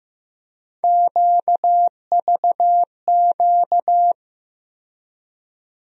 Morse code, heard as a keyed tone, about 15 wpm, 700 Hz. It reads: QVQ
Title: USA Call Sign Postfixes - 3 Letters 15wpm